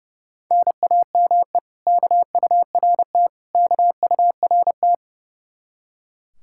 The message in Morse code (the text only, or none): NAME KURT KURT